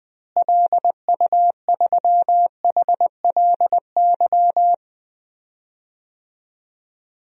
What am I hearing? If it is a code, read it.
Morse code, heard as a keyed tone, about 20 wpm, 700 Hz. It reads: LU3HLY
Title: DXCC-100 Call Signs - 6 Characters - Repeated 3x - 20wpm